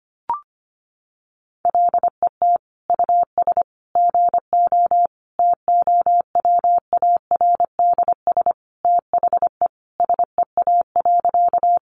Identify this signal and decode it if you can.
Morse code, heard as a keyed tone, about 25 wpm, 700 Hz. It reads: LET US GO TOWARDS THE SEA.
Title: Sentences from Top 1100 Words 25wpm